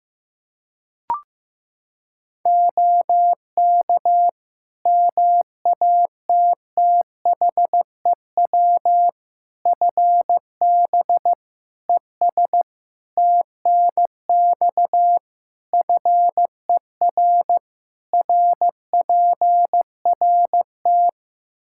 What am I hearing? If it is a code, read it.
Morse code, heard as a keyed tone, about 15 wpm, 700 Hz. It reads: OK MATTHEW FB ES TNX FER RPRT
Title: Practice QSO - Exchange 2 - Info - 15wpm